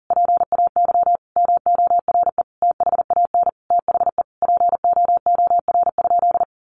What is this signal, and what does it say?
Morse code, heard as a keyed tone, about 40 wpm, 700 Hz. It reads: PAY MORE THAN THE POOR?
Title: Sentences from Top 1100 Words 40wpm